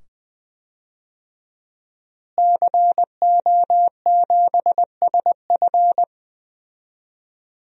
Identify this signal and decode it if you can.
Morse code, heard as a keyed tone, about 20 wpm, 700 Hz. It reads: CO7SF
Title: DXCC-100 Call Signs - 5 Characters - Repeated 2x - 20wpm